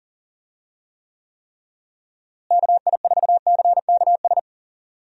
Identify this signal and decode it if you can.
Morse code, heard as a keyed tone, about 40 wpm, 700 Hz. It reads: KI4CKS
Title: USA Call Signs - 6 Characters - Repeated 3x - 40wpm